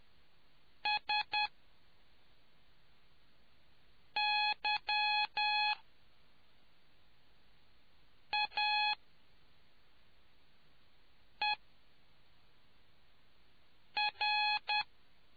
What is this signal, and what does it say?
Morse code, heard as a keyed tone, about 10 wpm, spaced as for 3 wpm, 2500 Hz. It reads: SYAER